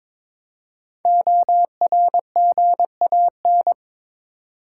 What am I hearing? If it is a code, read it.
Morse code, heard as a keyed tone, about 23 wpm, 700 Hz. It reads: ORGAN
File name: 5 Letter Words - First 3 Letters Given - 22wpm